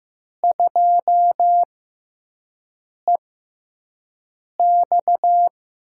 Morse code, heard as a keyed tone, about 15 wpm, spaced as for 5 wpm, 700 Hz. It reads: 2EX
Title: USA Call Signs - 6x Character Spacing - 15wpm